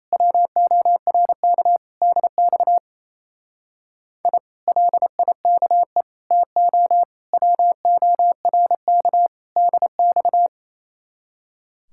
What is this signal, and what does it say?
Morse code, heard as a keyed tone, about 33 wpm, 700 Hz. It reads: WORK DX I LIKE TO WORK DX
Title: Speed-Racing Common QSO Phrases Encore - 28wpm